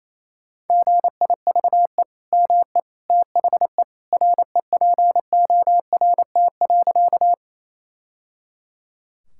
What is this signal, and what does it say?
Morse code, heard as a keyed tone, about 28 wpm, 700 Hz. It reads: GIVE ME THE REPORT.
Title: Sentences from Top 1000 Words - Review Part 6 - 28wpm